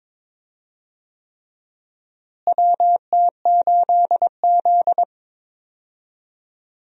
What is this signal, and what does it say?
Morse code, heard as a keyed tone, about 22 wpm, 700 Hz. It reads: WT8Z
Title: USA Call Signs - 4 Characters - Repeated 3x - 22wpm